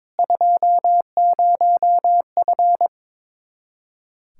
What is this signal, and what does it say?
Morse code, heard as a keyed tone, about 22 wpm, 700 Hz. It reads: 20F
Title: Practice QSO - Exchange 2 - Info - 22wpm